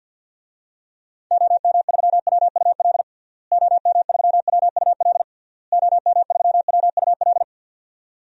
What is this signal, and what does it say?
Morse code, heard as a keyed tone, about 49 wpm, 700 Hz. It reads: OM3WUL OM3WUL OM3WUL
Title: DXCC-100 Call Signs - 6 Characters - Repeated 3x - 50wpm